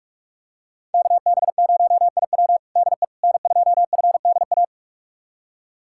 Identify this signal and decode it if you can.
Morse code, heard as a keyed tone, about 45 wpm, 690 Hz. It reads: KB0IW DE N2FDA
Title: USA Callsign DE Callsign 45wpm